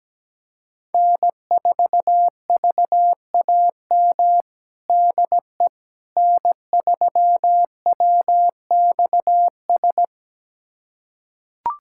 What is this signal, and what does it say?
Morse code, heard as a keyed tone, about 17 wpm, 700 Hz. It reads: N4VAM DE N3WXS
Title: USA Callsign DE Callsign - 5 Characters - 17wpm